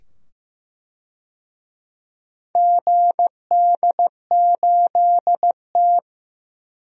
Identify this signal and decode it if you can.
Morse code, heard as a keyed tone, about 15 wpm, 700 Hz. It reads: GD8T
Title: DXCC-100 Call Signs - 4 Characters - 15wpm